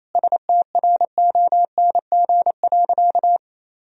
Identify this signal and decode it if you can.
Morse code, heard as a keyed tone, about 28 wpm, 700 Hz. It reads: STRONG.
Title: Sentences from Top 1000 Words - Review Part 4 - 28wpm